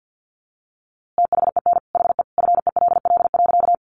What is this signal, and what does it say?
Morse code, heard as a keyed tone, about 50 wpm, 700 Hz. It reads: THEN HE FELL.
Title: A Sound Of Thunder 50wpm